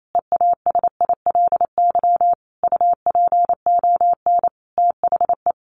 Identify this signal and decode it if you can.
Morse code, heard as a keyed tone, about 28 wpm, 700 Hz. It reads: EASILY UPON THE
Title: A Sound Of Thunder 28wpm